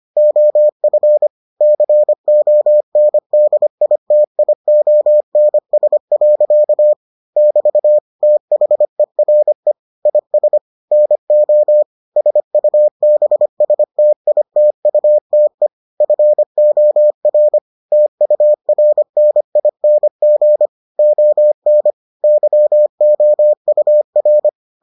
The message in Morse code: OF CONDITIONS. <BT> THERE IS NO SUBSTITUTE FOR TURNING ON YOUR

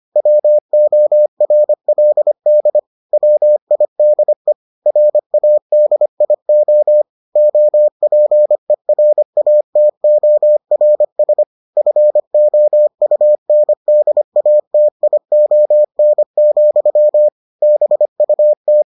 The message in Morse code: WORLD WIDE RADIO OPERATORS FOUNDATION, BUT

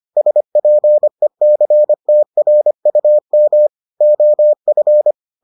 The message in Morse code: SPECTRUM OF

